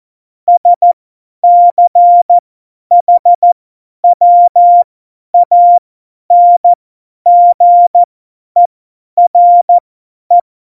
Morse code, heard as a keyed tone, about 14 wpm, spaced as for 10 wpm, 700 Hz. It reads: SCHWANGERE
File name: Morsekurs / Witze / Witz 28. August 2018, 50 F70